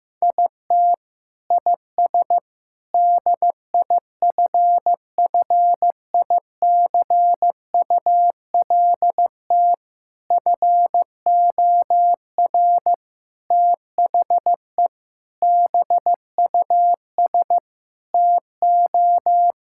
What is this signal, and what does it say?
Morse code, heard as a keyed tone, about 15 wpm, 700 Hz. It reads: IT IS DIFFICULT FOR THE BUS TO
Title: Sentences from Top 1000 Words - Review Part 6 - 15wpm